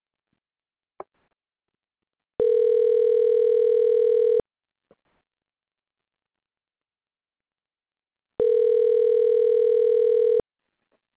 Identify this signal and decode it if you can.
Ringback tone (North America), 440 + 480 Hz.
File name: recordings / REac0722eeea6b2cc8fe761fd7295af439